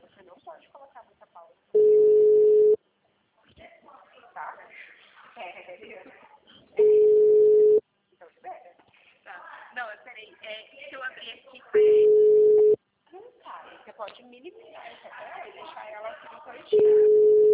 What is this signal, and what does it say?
Ringback tone (Europe, 425 Hz), with speech.
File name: out- -2004- - - .90118